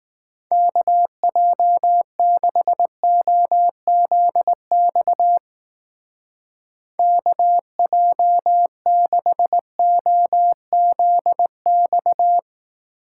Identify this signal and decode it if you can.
Morse code, heard as a keyed tone, about 20 wpm, 700 Hz. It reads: KJ6OZX KJ6OZX